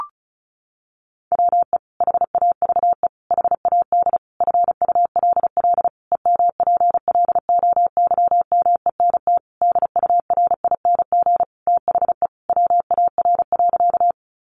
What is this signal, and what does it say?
Morse code, heard as a keyed tone, about 34 wpm, 700 Hz. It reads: WE HAVE HAD FULL EMPLOYMENT DURING THE WAR.
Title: Sentences from Top 1500 Words 35wpm